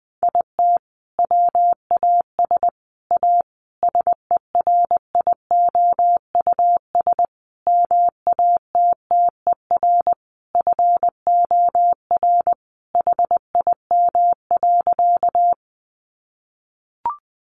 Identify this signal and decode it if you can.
Morse code, heard as a keyed tone, about 20 wpm, 700 Hz. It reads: IT WAS A SERIOUS MATTER FOR HIM.
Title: Sentences from Top 1100 Words 20wpm